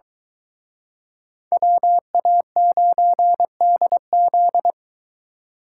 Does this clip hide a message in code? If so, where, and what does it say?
Yes: Morse code from 1.5 to 4.7 s, heard as a keyed tone, about 25 wpm, 700 Hz. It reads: WA9DZ